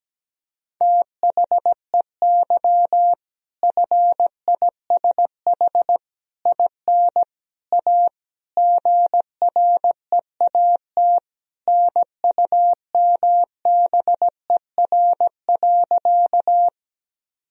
Morse code, heard as a keyed tone, about 17 wpm, 700 Hz. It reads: THEY FISH IN A GREAT NUMBER.
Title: Sentences from Top 1000 Words - Review Part 1 - 17wpm